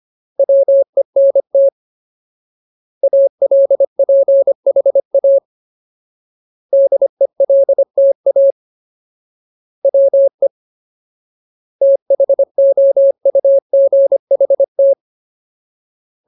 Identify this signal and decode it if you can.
Morse code, heard as a keyed tone, about 25 wpm, 550 Hz. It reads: WENT ALPHA DELTA WE THOUGHT
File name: Ultimate Word List / 25WPM Ultimate